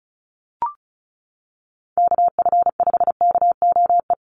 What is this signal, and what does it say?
Morse code, heard as a keyed tone, about 34 wpm, 700 Hz. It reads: KF5KOE
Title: USA Call Signs Repeated 3x 35wpm